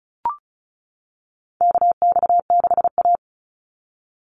Morse code, heard as a keyed tone, about 36 wpm, 700 Hz. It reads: KX6A
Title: NA SOTA Chaser Callsigns 35wpm